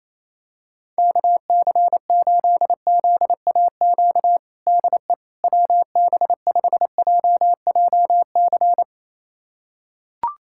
Morse code, heard as a keyed tone, about 28 wpm, 700 Hz. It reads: KC8ZAQ DE WB5JJC
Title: USA Callsign DE Callsign 28wpm